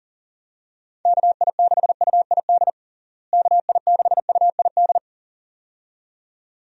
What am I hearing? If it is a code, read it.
Morse code, heard as a keyed tone, about 40 wpm, 700 Hz. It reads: KI6UID KI6UID
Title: USA Call Signs - 6 Characters - Repeated 2x - 40wpm